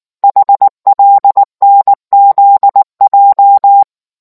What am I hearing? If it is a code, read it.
Morse code, heard as a keyed tone, about 19 wpm, 800 Hz. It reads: HLNZJ